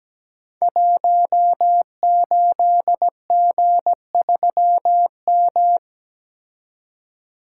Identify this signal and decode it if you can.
Morse code, heard as a keyed tone, about 17 wpm, 700 Hz. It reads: 18G3M
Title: Code Groups - 5 Letter:Numbers - 17wpm